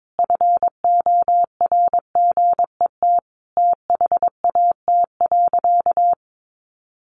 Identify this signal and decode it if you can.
Morse code, heard as a keyed tone, about 22 wpm, 700 Hz. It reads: FORGET THAT.
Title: Sentences from Top 1200 Words 22wpm